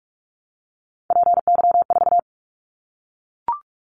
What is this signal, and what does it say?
Morse code, heard as a keyed tone, about 44 wpm, 700 Hz. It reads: PY4
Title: Callsign Prefixes for Top 50 DXCC - 45wpm